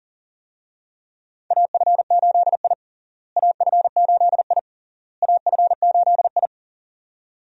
Morse code, heard as a keyed tone, about 40 wpm, 700 Hz. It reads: AF8I AF8I AF8I
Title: USA Call Signs - 4 Characters - Repeated 3x - 40wpm